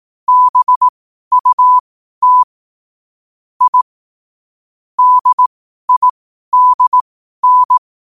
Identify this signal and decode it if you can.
Morse code, heard as a keyed tone, about 18 wpm, spaced as for 12 wpm, 1000 Hz. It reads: BUT I DIDN
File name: qotdfiles / 12wpm